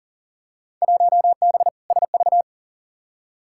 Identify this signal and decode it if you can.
Morse code, heard as a keyed tone, about 40 wpm, 700 Hz. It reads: 1B SV
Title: Field Day Exchanges Encore - 40wpm